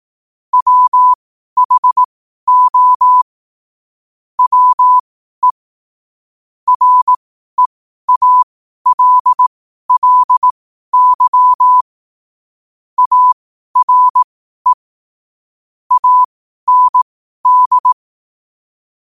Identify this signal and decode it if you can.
Morse code, heard as a keyed tone, about 18 wpm, spaced as for 12 wpm, 1000 Hz. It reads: WHO WE REALLY ARE AND
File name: qotdfiles / 12wpm